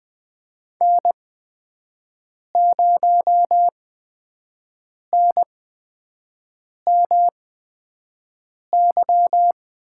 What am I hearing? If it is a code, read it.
Morse code, heard as a keyed tone, about 20 wpm, spaced as for 5 wpm, 700 Hz. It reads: N0NMY
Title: USA Call Signs - 8x Character Spacing - 20wpm